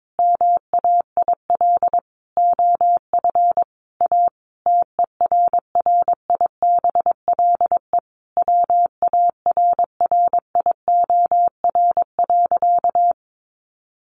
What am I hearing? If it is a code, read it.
Morse code, heard as a keyed tone, about 22 wpm, 700 Hz. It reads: MAIL OF A TERRIBLE WARRIOR.
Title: A Sound Of Thunder 22wpm